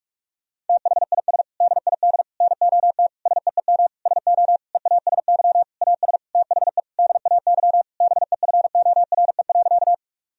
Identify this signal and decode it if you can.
Morse code, heard as a keyed tone, about 43 wpm, 700 Hz. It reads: THIS DID NOT SEEM SO EASY AS THE DAY BEFORE.
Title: Sentences from Top 1000 Words - Review Part 1 - 45wpm